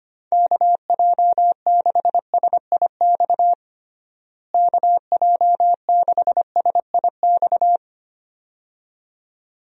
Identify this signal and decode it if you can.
Morse code, heard as a keyed tone, about 25 wpm, 700 Hz. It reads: KJ6SIX KJ6SIX
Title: USA Call Signs - 6 Characters - Repeated 2x - 25wpm